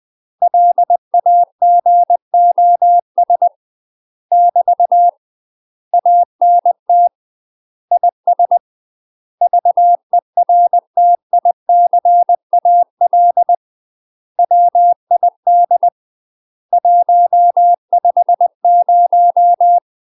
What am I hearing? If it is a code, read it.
Morse code, heard as a keyed tone, about 20 wpm, 700 Hz. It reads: LAGOS <BT> ANT IS VERTICAL WID 150
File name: UK MASTER QSOs TRAINING / 20WPM / UK MASTER QSOs 0032